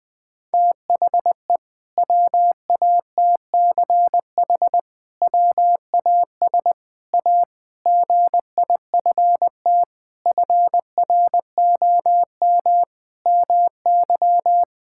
Morse code, heard as a keyed tone, about 20 wpm, 700 Hz. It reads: THE WATCH WAS A GIFT FROM MY
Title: Sentences from Top 1000 Words - Review Part 6 - 20wpm